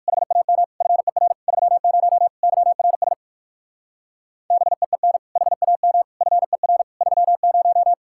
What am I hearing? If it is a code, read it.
Morse code, heard as a keyed tone, about 59 wpm, 700 Hz. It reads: HAM FER 30 YRS BEEN HAM FER 30